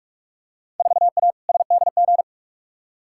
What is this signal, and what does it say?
Morse code, heard as a keyed tone, about 46 wpm, 700 Hz. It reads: 4A SDG
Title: Field Day Exchanges Encore - 45wpm